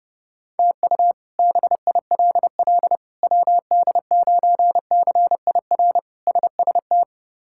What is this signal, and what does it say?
Morse code, heard as a keyed tone, about 30 wpm, 700 Hz. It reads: TU BILL WD9CIR SST
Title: K1USN SST QSO Practice - 30wpm